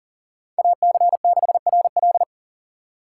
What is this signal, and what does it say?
Morse code, heard as a keyed tone, about 40 wpm, 700 Hz. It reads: AC6RL
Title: USA Call Signs - 5 Characters - 3x Character Spacing - 40wpm